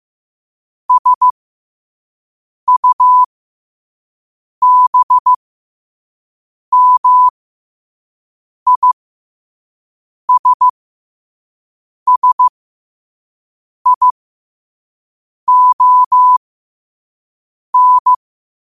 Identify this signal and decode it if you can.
Morse code, heard as a keyed tone, about 15 wpm, spaced as for 5 wpm, 1000 Hz. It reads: SUBMISSION
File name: qotdfiles / 5wpm